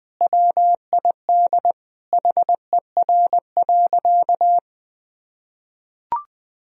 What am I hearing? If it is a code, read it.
Morse code, heard as a keyed tone, about 20 wpm, 700 Hz. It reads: WID HER.
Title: Sentences from Top 100 Words and Abbreviations 20wpm